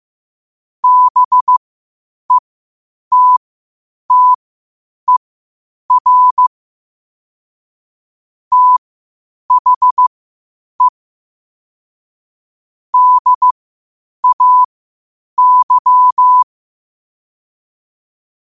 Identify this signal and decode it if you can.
Morse code, heard as a keyed tone, about 15 wpm, spaced as for 8 wpm, 1000 Hz. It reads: BETTER THE DAY
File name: qotdfiles / 8wpm